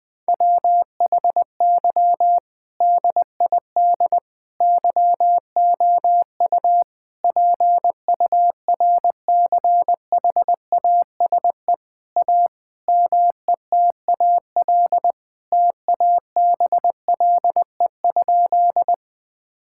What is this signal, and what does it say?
Morse code, heard as a keyed tone, about 20 wpm, 700 Hz. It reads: WHY DID YOU PURCHASE A METAL TABLE?
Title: Sentences from Top 1000 Words - Review Part 6 - 20wpm